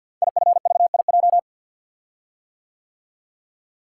Morse code, heard as a keyed tone, about 49 wpm, 700 Hz. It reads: IFVIJ
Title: Code Groups - 5 Letters - 50wpm